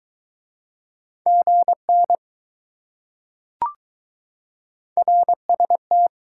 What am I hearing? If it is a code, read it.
Morse code, heard as a keyed tone, about 21 wpm, 700 Hz. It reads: GN RST